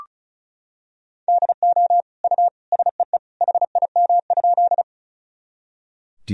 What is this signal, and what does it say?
Morse code, heard as a keyed tone, about 36 wpm, 700 Hz. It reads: DO U SEE HIM?
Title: Sentences from Top 100 Words and Abbreviations 35wpm